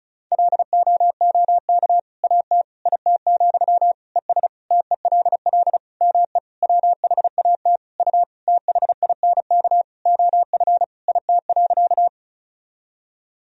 Morse code, heard as a keyed tone, about 34 wpm, 700 Hz. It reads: LOOK AT IT, ES TELL ME WHAT U THINK OF IT.
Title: Sentences from Top 200 Words and Abbreviations 35wpm